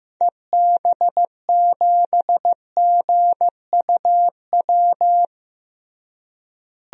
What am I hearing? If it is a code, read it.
Morse code, heard as a keyed tone, about 15 wpm, 700 Hz. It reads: EB7GUW
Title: DXCC-100 Call Signs - 6 Characters - Repeated 2x - 15wpm